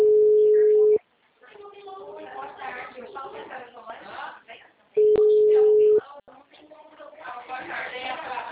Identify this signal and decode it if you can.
Ringback tone (Europe, 425 Hz), with speech.